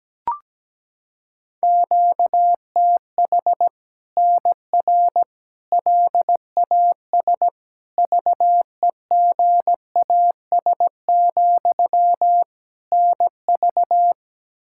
Morse code, heard as a keyed tone, about 17 wpm, 700 Hz. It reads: QTH NR LAS VEGAS, NV
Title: Practice QSO - Exchange 1 - Beginning - 17wpm